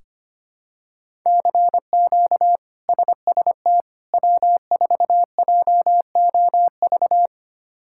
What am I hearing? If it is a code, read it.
Morse code, heard as a keyed tone, about 25 wpm, 700 Hz. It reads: CQ SST W4JOV
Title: K1USN SST QSO Practice - 25wpm